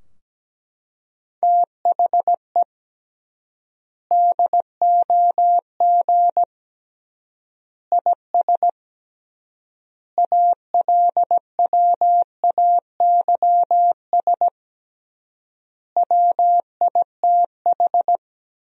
Morse code, heard as a keyed tone, about 17 wpm, 700 Hz. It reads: THE DOG IS ALWAYS WITH
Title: 3x Word-Spacing - Sentences from Top 400 Words - 17wpm